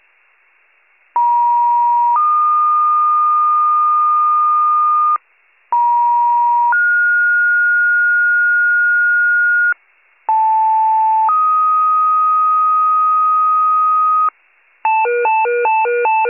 Two-tone page, heard as a one-second tone, then a three-second tone, about 950 then 1220 Hz, about 950 then 1430 Hz, about 880 then 1250 Hz, with noise. It is heard from 1.2 to 14.3 s.